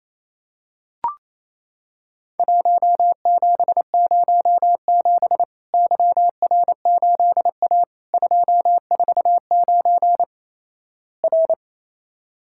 Morse code, heard as a keyed tone, about 28 wpm, 700 Hz. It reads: 1707 YR8A 249 R
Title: Worked All Europe DX Contest - QTC Exchange - 28wpm